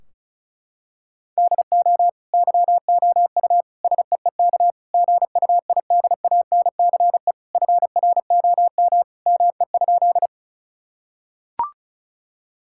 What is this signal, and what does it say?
Morse code, heard as a keyed tone, about 36 wpm, 700 Hz. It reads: DO YOU SEEK GUIDANCE FROM ME?
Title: Sentences from Top 1600 Words 35wpm